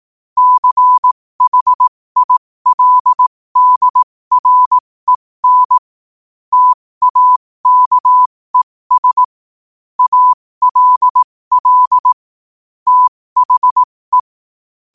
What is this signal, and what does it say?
Morse code, heard as a keyed tone, about 18 wpm, spaced as for 15 wpm, 1000 Hz. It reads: CHILDREN TAKES ALL THE
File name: qotdfiles / 15wpm